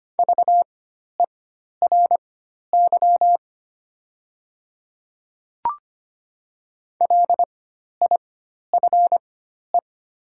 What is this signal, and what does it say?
Morse code, heard as a keyed tone, about 25 wpm, spaced as for 12 wpm, 700 Hz. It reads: VERY LIFE
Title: 4x Character Spacing - Top 300 Words - 25wpm